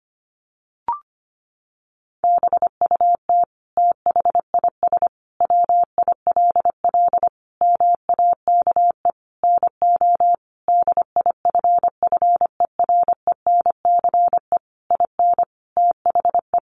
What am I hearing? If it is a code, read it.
Morse code, heard as a keyed tone, about 25 wpm, 700 Hz. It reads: BUT THIS WILL MAKE NO DIFFERENCE IN THE
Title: Sentences from Top 800 Words 25wpm